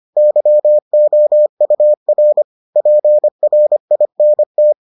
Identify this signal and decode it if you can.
Morse code, heard as a keyed tone, about 25 wpm, 600 Hz. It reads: YOUR PRINT